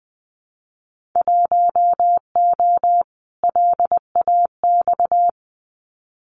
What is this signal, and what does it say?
Morse code, heard as a keyed tone, about 20 wpm, 700 Hz. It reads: 1O LAX
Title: Winter Field Day Exchanges 20wpm